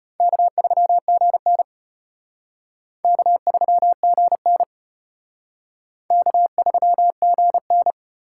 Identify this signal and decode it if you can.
Morse code, heard as a keyed tone, about 32 wpm, 700 Hz. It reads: K3GN K3GN K3GN